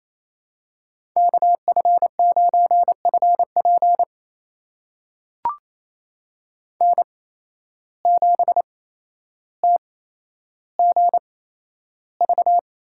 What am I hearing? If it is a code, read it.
Morse code, heard as a keyed tone, about 28 wpm, spaced as for 12 wpm, 700 Hz. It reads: KF9FP N7TGV
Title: USA Call Signs - 5 Characters - 8x Character Spacing - 28wpm